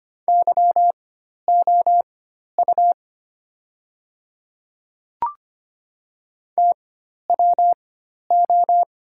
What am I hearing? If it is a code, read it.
Morse code, heard as a keyed tone, about 25 wpm, spaced as for 12 wpm, 700 Hz. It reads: YOU TWO
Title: 4x Character Spacing - Top 100 Words - 25wpm